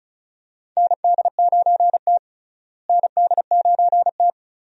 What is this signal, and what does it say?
Morse code, heard as a keyed tone, about 34 wpm, 700 Hz. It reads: ND9T ND9T
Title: USA Call Signs - 4 Characters - Repeated 2x - 35wpm